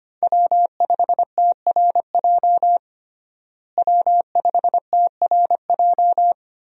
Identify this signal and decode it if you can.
Morse code, heard as a keyed tone, about 25 wpm, 700 Hz. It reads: W5TRJ W5TRJ
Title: USA Call Signs - 5 Characters - Repeated 3x - 25wpm